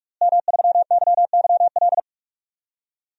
Morse code, heard as a keyed tone, about 46 wpm, 700 Hz. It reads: M3YYL
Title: DXCC-100 Call Signs - 5 Characters - Repeated 2x - 45wpm